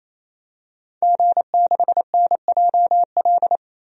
Morse code, heard as a keyed tone, about 28 wpm, 700 Hz. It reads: G6NJL
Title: DXCC-100 Call Signs - 5 Characters - 28wpm